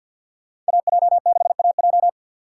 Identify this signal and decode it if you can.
Morse code, heard as a keyed tone, about 63 wpm, 700 Hz. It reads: AJ6AJ